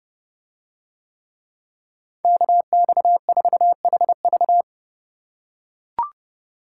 Morse code, heard as a keyed tone, about 28 wpm, 700 Hz. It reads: KX4HV